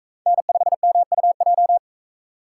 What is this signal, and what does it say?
Morse code, heard as a keyed tone, about 42 wpm, 700 Hz. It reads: N5MUJ